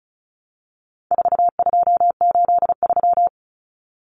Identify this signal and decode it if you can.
Morse code, heard as a keyed tone, about 34 wpm, 700 Hz. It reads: 4283
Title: Common Cut Numbers - 4-Digit 35wpm